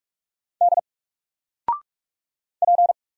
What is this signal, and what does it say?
Morse code, heard as a keyed tone, about 45 wpm, 700 Hz. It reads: D P
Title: Single Letter-Number Alt 45wpm